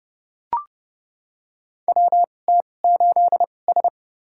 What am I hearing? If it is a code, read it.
Morse code, heard as a keyed tone, about 30 wpm, spaced as for 22 wpm, 700 Hz. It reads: WT8S
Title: USA Call Signs - 4 Characters - 2x Character Spacing - 30wpm